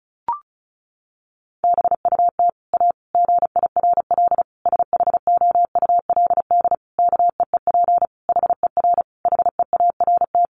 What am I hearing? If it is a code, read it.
Morse code, heard as a keyed tone, about 34 wpm, 700 Hz. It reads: BUT A GIRL SHOULD KEEP HER HEART
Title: Sentences from Top 1200 Words 35wpm